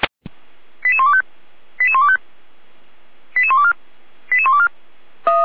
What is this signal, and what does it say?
Five-tone selective call (ZVEI), heard as quick runs of five tones, tones of about 70 ms. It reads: 88125 88125 88124 88124